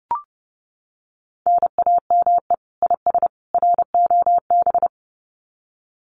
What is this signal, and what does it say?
Morse code, heard as a keyed tone, about 30 wpm, 700 Hz. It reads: NAME IS ROB
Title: Common QSO Phrases 30wpm